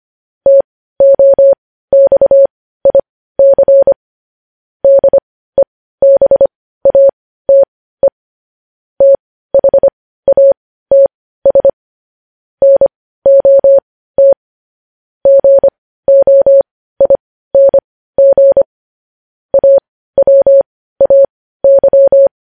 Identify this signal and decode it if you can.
Morse code, heard as a keyed tone, about 25 wpm, spaced as for 15 wpm, 550 Hz. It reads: TOXIC DEBATE THATS NOT GOING AWAY
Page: news-headlines-in-morse-code-at-15-wpm